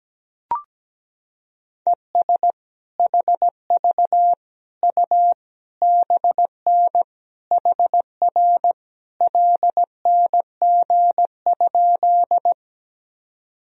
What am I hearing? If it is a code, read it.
Morse code, heard as a keyed tone, about 17 wpm, 700 Hz. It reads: ES HV U BN HR LNG?
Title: Sentences from Top 200 Words and Abbreviations 17wpm